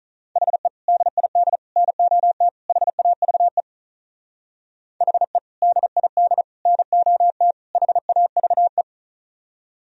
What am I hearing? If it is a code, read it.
Morse code, heard as a keyed tone, about 44 wpm, 700 Hz. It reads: HE DID NOT HAVE HE DID NOT HAVE